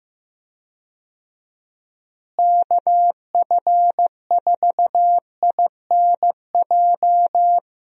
Morse code, heard as a keyed tone, about 15 wpm, 700 Hz. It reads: KF4INJ
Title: USA Call Signs - 6 Characters - Repeated 3x - 15wpm